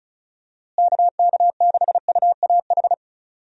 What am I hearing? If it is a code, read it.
Morse code, heard as a keyed tone, about 36 wpm, 700 Hz. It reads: KK6UAH
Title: USA Call Signs - 6 Characters - Repeated 3x - 35wpm